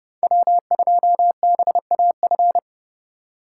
Morse code, heard as a keyed tone, about 30 wpm, 700 Hz. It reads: W2BAF
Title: USA Call Signs - 3x Character Spacing - 30wpm